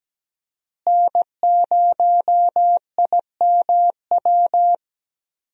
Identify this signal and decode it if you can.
Morse code, heard as a keyed tone, about 17 wpm, 700 Hz. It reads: N0IMW
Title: USA Call Signs - 5 Characters - Repeated 2x - 17wpm